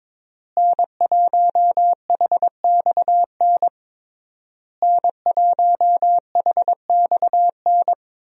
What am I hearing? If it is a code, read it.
Morse code, heard as a keyed tone, about 22 wpm, 700 Hz. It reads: N1HXN N1HXN
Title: USA Call Signs - 5 Characters - Repeated 3x - 22wpm